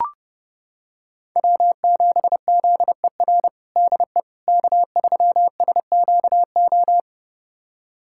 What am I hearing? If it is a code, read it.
Morse code, heard as a keyed tone, about 30 wpm, 700 Hz. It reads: W7ZER DE K3SQO
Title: USA Callsign DE Callsign - 5 Characters - 30wpm